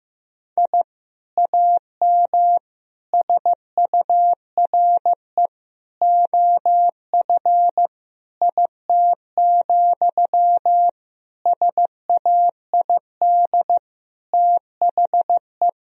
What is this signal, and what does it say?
Morse code, heard as a keyed tone, about 15 wpm, 700 Hz. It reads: I AM SURE OF IT, SAID THE
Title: Sentences from Top 1000 Words - Review Part 4 - 15wpm